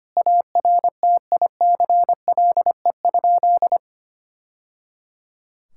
Morse code, heard as a keyed tone, about 25 wpm, 700 Hz. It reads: ARTICLE?
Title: Sentences from Top 1000 Words - Review Part 5 - 25wpm